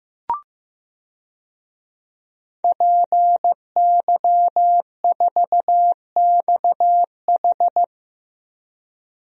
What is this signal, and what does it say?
Morse code, heard as a keyed tone, about 15 wpm, 700 Hz. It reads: PY4XH
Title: DXCC-100 Call Signs - 5 Characters - Repeated 3x - 15wpm